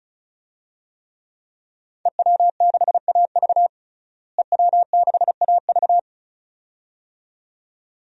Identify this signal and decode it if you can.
Morse code, heard as a keyed tone, about 34 wpm, 700 Hz. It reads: EW6AV EW6AV
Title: DXCC-100 Call Signs - 5 Characters - Repeated 2x - 35wpm